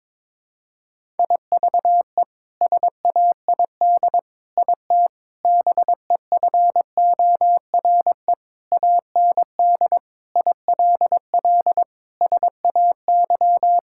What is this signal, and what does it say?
Morse code, heard as a keyed tone, about 22 wpm, 700 Hz. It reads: IVE SAID IT BEFORE AND ILL SAY
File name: practice / 22wpm / Sets of 8 Words - 22wpm